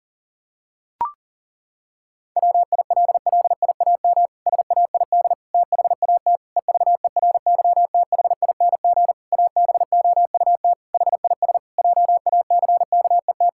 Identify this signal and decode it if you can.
Morse code, heard as a keyed tone, about 40 wpm, 700 Hz. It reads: WILLIAM SAID THAT EVERYTHING ABOUT HIS JACKET
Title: Pangram Challenge v2 - 40wpm